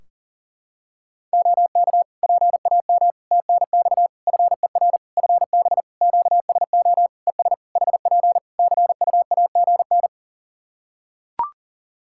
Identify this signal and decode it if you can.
Morse code, heard as a keyed tone, about 40 wpm, 700 Hz. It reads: OK PAM TNX FER FB QSO ES HP CUAGN
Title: K1USN SST QSO Practice 40wpm